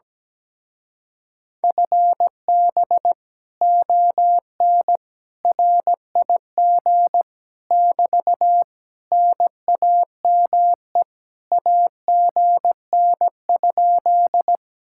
Morse code, heard as a keyed tone, about 17 wpm, 700 Hz. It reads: FB ON RIG <BT> NAME AGN?
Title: Speed-Racing Common QSO Phrases - 17wpm